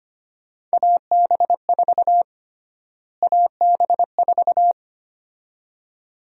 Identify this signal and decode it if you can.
Morse code, heard as a keyed tone, about 25 wpm, 700 Hz. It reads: AB4 AB4
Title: USA Call Sign Prefixes - Repeated 2x - 25wpm